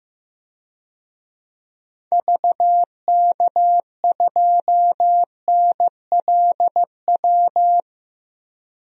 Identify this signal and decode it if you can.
Morse code, heard as a keyed tone, about 15 wpm, 700 Hz. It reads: VK2NLW
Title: DXCC-100 Call Signs - 6 Characters - 15wpm